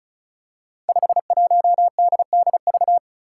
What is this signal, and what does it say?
Morse code, heard as a keyed tone, about 36 wpm, 700 Hz. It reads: 51DDV
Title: Five Challenging Letter-Number Combinations 35wpm